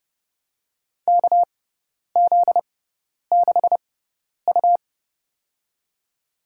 Morse code, heard as a keyed tone, about 30 wpm, spaced as for 10 wpm, 700 Hz. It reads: KZ6U
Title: USA Call Signs - 4 Characters - 6x Character Spacing - 30wpm